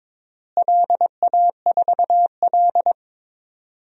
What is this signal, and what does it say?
Morse code, heard as a keyed tone, about 22 wpm, 700 Hz. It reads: LA4L
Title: DXCC-100 Call Signs - 4 Characters - Repeated 3x - 22wpm